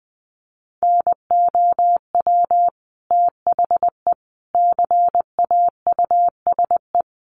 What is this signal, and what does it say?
Morse code, heard as a keyed tone, about 20 wpm, 700 Hz. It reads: NOW THE CAUSE
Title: Sentences from Top 1500 Words 20wpm